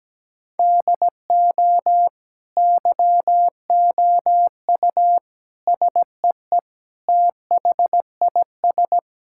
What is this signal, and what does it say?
Morse code, heard as a keyed tone, about 17 wpm, 700 Hz. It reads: DO YOU SEE THIS
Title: Sentences from Top 1600 Words 17wpm